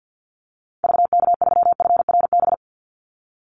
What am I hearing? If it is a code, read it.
Morse code, heard as a keyed tone, about 50 wpm, 700 Hz. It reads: VK3FRB
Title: USA Call Signs 50wpm